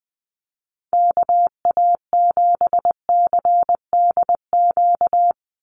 Morse code, heard as a keyed tone, about 20 wpm, 700 Hz. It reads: KA7CDQ
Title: USA Call Signs 20wpm